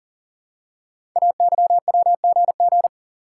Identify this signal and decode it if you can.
Morse code, heard as a keyed tone, about 38 wpm, 690 Hz. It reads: AYWGG
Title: Code Groups - 5 Letters - 40wpm